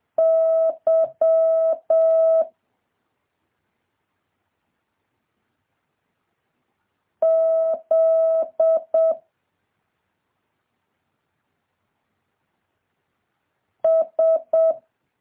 Morse code, heard as a keyed tone, about 7 wpm, 650 Hz. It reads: Y Z S
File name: jh8cbh / 2023cw